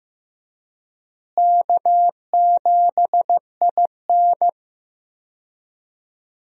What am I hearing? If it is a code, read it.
Morse code, heard as a keyed tone, about 15 wpm, 700 Hz. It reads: K7IN
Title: USA Call Signs - 4 Characters - Repeated 2x - 15wpm